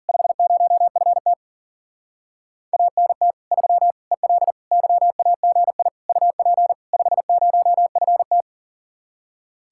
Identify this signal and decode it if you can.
Morse code, heard as a keyed tone, about 47 wpm, 700 Hz. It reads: 50FT ANT 3 EL YAGI UP 50FT
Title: Speed-Racing Common QSO Phrases - 40wpm